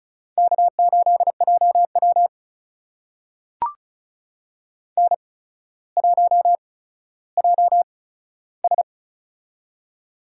Morse code, heard as a keyed tone, about 33 wpm, spaced as for 15 wpm, 700 Hz. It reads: K8JW N1JS
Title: USA Call Signs - 4 Characters - 8x Character Spacing - 35wpm